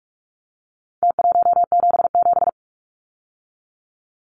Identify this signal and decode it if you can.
Morse code, heard as a keyed tone, about 44 wpm, 700 Hz. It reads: T177
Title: Numbers - Contest Style - 4-Digit 45wpm